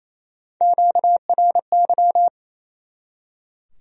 Morse code, heard as a keyed tone, about 36 wpm, 700 Hz. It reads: QRY